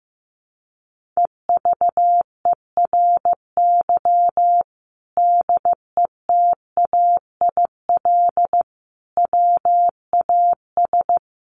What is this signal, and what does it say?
Morse code, heard as a keyed tone, about 14 wpm, 700 Hz. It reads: EVERY DETAIL WAS
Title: Sentences from Top 1500 Words 15wpm